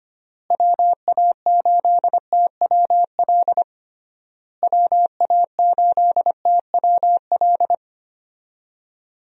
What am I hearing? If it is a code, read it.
Morse code, heard as a keyed tone, about 25 wpm, 700 Hz. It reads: WA8TWL WA8TWL
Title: USA Call Signs - 6 Characters - Repeated 3x - 25wpm